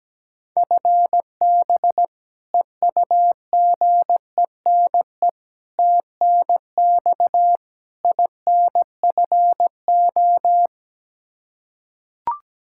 Morse code, heard as a keyed tone, about 17 wpm, 700 Hz. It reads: FB EUGENE TNX INFO
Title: Practice QSO - Exchange 3 - More Info - 17wpm